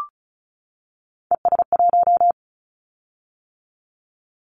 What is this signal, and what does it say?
Morse code, heard as a keyed tone, about 36 wpm, 700 Hz. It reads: ES1
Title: Callsign Prefixes for Top 50 DXCC - 35wpm